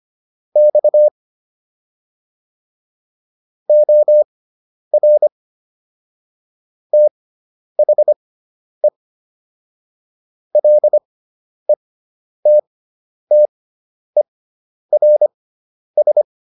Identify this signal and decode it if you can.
Morse code, heard as a keyed tone, about 25 wpm, spaced as for 10 wpm, 600 Hz. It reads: X OR THE LETTERS